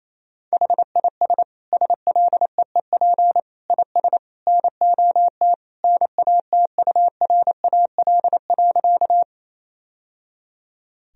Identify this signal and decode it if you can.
Morse code, heard as a keyed tone, about 28 wpm, 700 Hz. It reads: HIS SLEEP IS NOT NATURAL.
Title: Sentences from Top 1000 Words - Review Part 4 - 28wpm